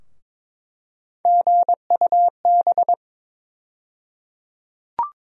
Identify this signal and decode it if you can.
Morse code, heard as a keyed tone, about 22 wpm, 700 Hz. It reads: GUB
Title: USA Call Sign Postfixes - 3 Letters 22wpm